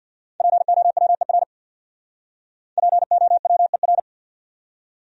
Morse code, heard as a keyed tone, about 63 wpm, 700 Hz. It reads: POWER POWER